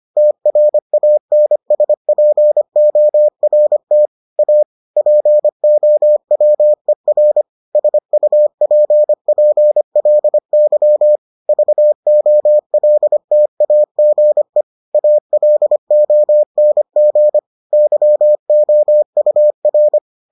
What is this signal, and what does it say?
Morse code, heard as a keyed tone, about 25 wpm, 600 Hz. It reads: TRANSPORT A POWER SUPPLY VOLTAGE ALONG YOUR